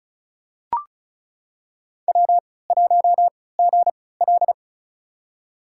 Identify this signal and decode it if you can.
Morse code, heard as a keyed tone, about 36 wpm, spaced as for 20 wpm, 700 Hz. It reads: W1GL
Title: USA Call Signs - 4 Characters - 3x Character Spacing - 35wpm